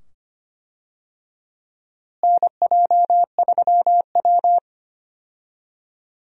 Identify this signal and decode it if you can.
Morse code, heard as a keyed tone, about 25 wpm, 700 Hz. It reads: NJ3W
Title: USA Call Signs - 4 Characters - Repeated 2x - 25wpm